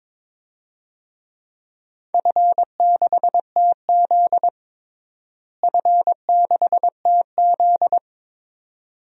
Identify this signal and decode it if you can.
Morse code, heard as a keyed tone, about 22 wpm, 700 Hz. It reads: F6TZ F6TZ
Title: DXCC-100 Call Signs - 4 Characters - Repeated 2x - 22wpm